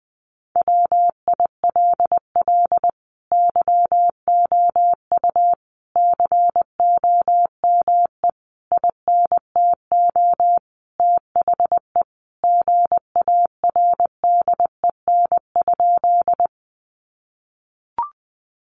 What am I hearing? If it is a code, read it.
Morse code, heard as a keyed tone, about 20 wpm, 700 Hz. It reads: WILL YOU COME INTO THE GARDEN?
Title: Sentences from Top 700 Words 20wpm